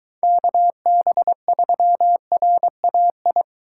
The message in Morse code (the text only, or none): KB3RAI